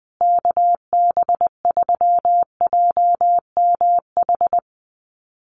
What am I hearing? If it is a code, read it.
Morse code, heard as a keyed tone, about 20 wpm, 700 Hz. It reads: KB3JMH
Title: USA Call Signs Repeated 3x 20wpm